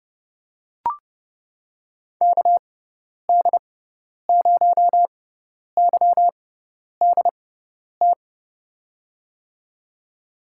Morse code, heard as a keyed tone, about 31 wpm, spaced as for 10 wpm, 700 Hz. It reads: KD0YDT
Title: USA Call Signs - 6 Characters - 6x Character Spacing - 30wpm